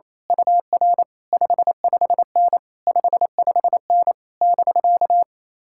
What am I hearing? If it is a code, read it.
Morse code, heard as a keyed tone, about 26 wpm, 700 Hz. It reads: UR 55N 55N <BK>
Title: Speed-Racing Common QSO Phrases - 28wpm